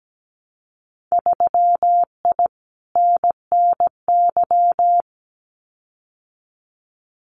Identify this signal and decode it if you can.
Morse code, heard as a keyed tone, about 17 wpm, 700 Hz. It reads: 3I NNY
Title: Winter Field Day Exchanges 17wpm